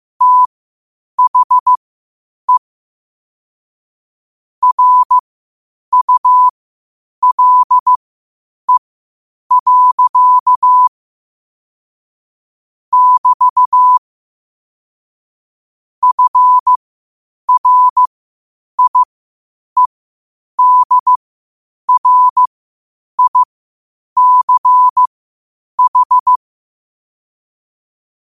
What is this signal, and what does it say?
Morse code, heard as a keyed tone, about 15 wpm, spaced as for 8 wpm, 1000 Hz. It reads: THE RULE. <BT> FRIEDRICH
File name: qotdfiles / 8wpm